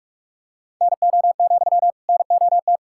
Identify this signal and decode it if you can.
Morse code, heard as a keyed tone, about 41 wpm, 700 Hz. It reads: NO, NOT
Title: Sentences from Top 1000 Words - Review Part 6 - 45wpm